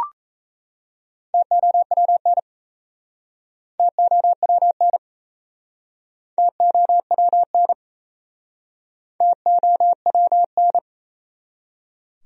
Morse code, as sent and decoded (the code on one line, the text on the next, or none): - --- .-- -. / - --- .-- -. / - --- .-- -. / - --- .-- -.
TOWN TOWN TOWN TOWN